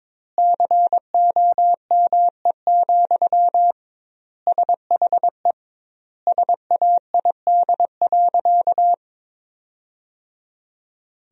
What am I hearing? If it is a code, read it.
Morse code, heard as a keyed tone, about 22 wpm, 700 Hz. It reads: COME, SHE SAID.
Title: 2x Word-Spacing - Sentences from Top 100 Words - 22wpm